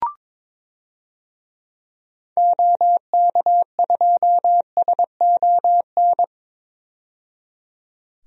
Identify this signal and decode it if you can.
Morse code, heard as a keyed tone, about 22 wpm, 700 Hz. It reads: OK2SON
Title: DXCC-100 Call Signs - 6 Characters - 22wpm